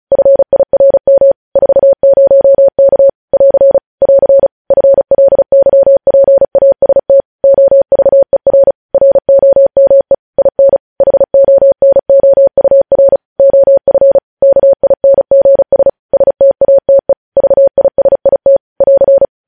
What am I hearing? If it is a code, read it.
Morse code, heard as a keyed tone, about 36 wpm, 550 Hz. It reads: FIRM 40K <AR> <AR> FLYPAST OVER ROME IN HONOUR OF KINGS STATE VISIT <AR>
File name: podcast / News headlines-1x-35x35